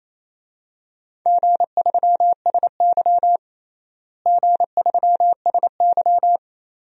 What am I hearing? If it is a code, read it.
Morse code, heard as a keyed tone, about 28 wpm, 700 Hz. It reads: G3SY G3SY
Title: DXCC-100 Call Signs - 4 Characters - Repeated 2x - 28wpm